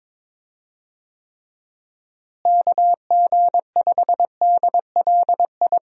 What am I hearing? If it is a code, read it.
Morse code, heard as a keyed tone, about 22 wpm, 700 Hz. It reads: KG5DLI
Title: USA Call Signs - 6 Characters - Repeated 3x - 22wpm